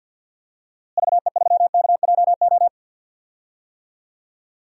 Morse code, heard as a keyed tone, about 46 wpm, 700 Hz. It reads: VE3KJO
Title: DXCC-100 Call Signs - 6 Characters - Repeated 2x - 50wpm